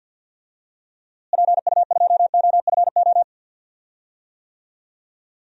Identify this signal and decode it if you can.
Morse code, heard as a keyed tone, about 51 wpm, 700 Hz. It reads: PU2OFO
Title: DXCC-100 Call Signs - 6 Characters - Repeated 3x - 50wpm